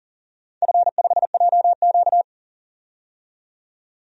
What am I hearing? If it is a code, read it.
Morse code, heard as a keyed tone, about 40 wpm, 700 Hz. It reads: F5JQ
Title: DXCC-100 Call Signs - 4 Characters - 40wpm